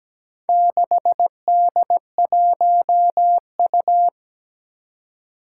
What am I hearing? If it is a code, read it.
Morse code, heard as a keyed tone, about 17 wpm, 700 Hz. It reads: 6D1U
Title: Four Challenging Letter-Number Combinations 17wpm